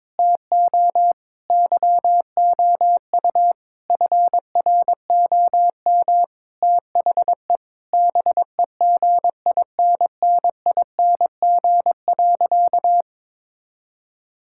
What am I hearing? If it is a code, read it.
Morse code, heard as a keyed tone, about 21 wpm, 700 Hz. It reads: TO YOU FROM THE BEGINNING.
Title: Sentences from Top 1000 Words - Review Part 5 - 22wpm